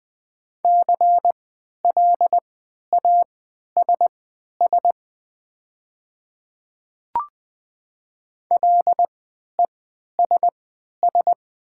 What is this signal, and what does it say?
Morse code, heard as a keyed tone, about 21 wpm, spaced as for 12 wpm, 700 Hz. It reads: CLASS LESS
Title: 3x Character Spacing - Top 500 Words - 20wpm